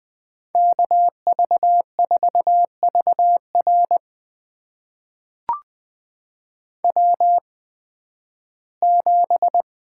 Morse code, heard as a keyed tone, about 20 wpm, spaced as for 13 wpm, 700 Hz. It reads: KV4VR W7
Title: USA Call Signs - 5 Characters - 8x Character Spacing - 20wpm